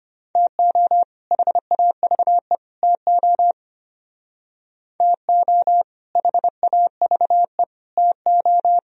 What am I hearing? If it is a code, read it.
Morse code, heard as a keyed tone, about 30 wpm, 700 Hz. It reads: TO HAVE TO TO HAVE TO
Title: Speed-Racing - Sets of 3 Words 25wpm